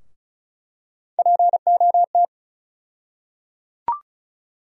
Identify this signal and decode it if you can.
Morse code, heard as a keyed tone, about 34 wpm, 700 Hz. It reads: POT
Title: USA Call Sign Postfixes - 3 Letters 35wpm